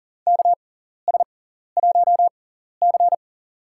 Morse code, heard as a keyed tone, about 37 wpm, spaced as for 14 wpm, 700 Hz. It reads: KS1C
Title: USA Call Signs - 6x Character Spacing - 40wpm